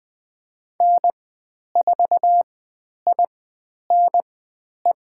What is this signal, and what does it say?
Morse code, heard as a keyed tone, about 20 wpm, spaced as for 10 wpm, 700 Hz. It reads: N4INE
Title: 102 Lesson 17g - Letter u - Call Signs 20wpm